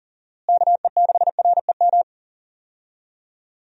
Morse code, heard as a keyed tone, about 37 wpm, 700 Hz. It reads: KE6REM
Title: USA Call Signs - 6 Characters - Repeated 2x - 40wpm